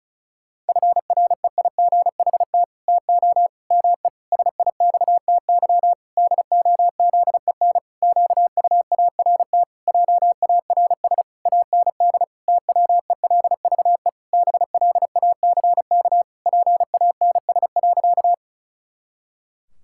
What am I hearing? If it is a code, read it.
Morse code, heard as a keyed tone, about 36 wpm, 700 Hz. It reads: FREIGHT TO ME SIXTY DOZEN QUART JARS AND TWELVE BLACK PANS.
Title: Pangram Challenge v2 - 35wpm